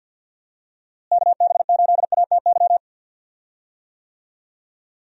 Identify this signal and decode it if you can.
Morse code, heard as a keyed tone, about 53 wpm, 700 Hz. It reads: KB8ATY
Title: USA Call Signs - 6 Characters - Repeated 2x - 50wpm